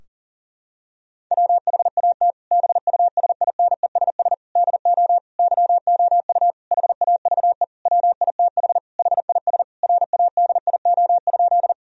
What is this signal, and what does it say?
Morse code, heard as a keyed tone, about 40 wpm, 700 Hz. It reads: WHAT BUSINESS DO YOU HAVE WITH HIS RADIO?
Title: Sentences from Top 1000 Words - Review Part 3 - 40wpm